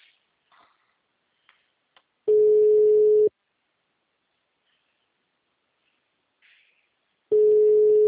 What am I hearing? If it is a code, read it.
Ringback tone (Europe, 425 Hz).